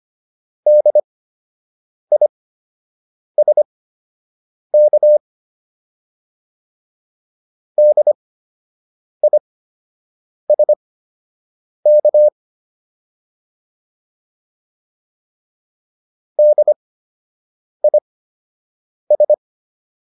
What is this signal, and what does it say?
Morse code, heard as a keyed tone, about 25 wpm, spaced as for 7 wpm, 600 Hz. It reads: DISK DISK DIS